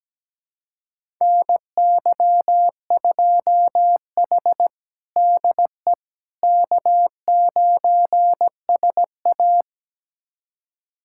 Morse code, heard as a keyed tone, about 17 wpm, 700 Hz. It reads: NY2H DE K9SA
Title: USA Callsign DE Callsign - 4 Characters - 17wpm